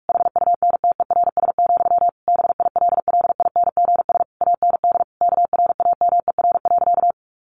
Morse code, heard as a keyed tone, about 44 wpm, 700 Hz. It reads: HUNTERS, BILLINGS AND KRAMER.
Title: A Sound Of Thunder 45wpm